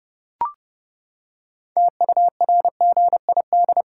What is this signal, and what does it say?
Morse code, heard as a keyed tone, about 30 wpm, 700 Hz. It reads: TURGID
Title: Pangram Challenge v2 - 30wpm